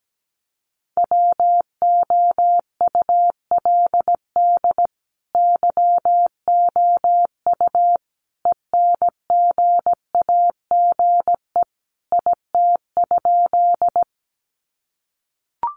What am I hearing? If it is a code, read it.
Morse code, heard as a keyed tone, about 17 wpm, 700 Hz. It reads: WOULD YOU ENGAGE IT?
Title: Sentences from Top 1500 Words 17wpm